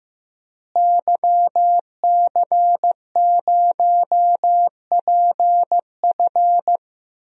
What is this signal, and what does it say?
Morse code, heard as a keyed tone, about 15 wpm, 700 Hz. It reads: YC0PF
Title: DXCC-100 Call Signs - 5 Characters - Repeated 2x - 15wpm